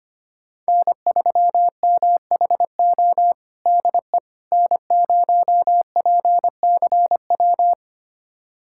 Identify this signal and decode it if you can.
Morse code, heard as a keyed tone, about 25 wpm, 700 Hz. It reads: N3MHO DE N0PCW
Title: USA Callsign DE Callsign - 5 Characters - 25wpm